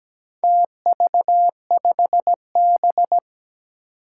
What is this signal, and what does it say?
Morse code, heard as a keyed tone, about 17 wpm, 700 Hz. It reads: TV5B
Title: Four Challenging Letter-Number Combinations 17wpm